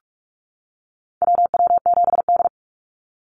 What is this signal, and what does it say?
Morse code, heard as a keyed tone, about 44 wpm, 700 Hz. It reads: WW7D
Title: NA SOTA Chaser Callsigns 45wpm